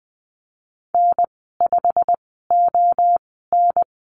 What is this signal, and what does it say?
Morse code, heard as a keyed tone, about 20 wpm, spaced as for 15 wpm, 700 Hz. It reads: N5ON
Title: 128 Lesson 21g - Number 6 - Call Signs 20wpm